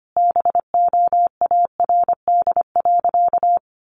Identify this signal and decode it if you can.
Morse code, heard as a keyed tone, about 25 wpm, 700 Hz. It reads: BOARD.
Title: Sentences from Top 700 Words 25wpm